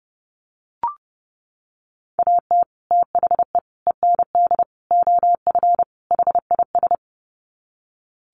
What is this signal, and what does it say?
Morse code, heard as a keyed tone, about 30 wpm, 700 Hz. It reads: AT THE END OF HIS
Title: Sets of 5 Words Encore 30wpm